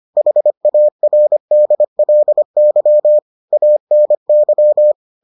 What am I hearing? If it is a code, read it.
Morse code, heard as a keyed tone, about 25 wpm, 600 Hz. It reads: HARDLY ANY